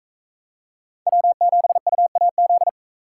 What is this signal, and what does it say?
Morse code, heard as a keyed tone, about 41 wpm, 700 Hz. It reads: W7UAZ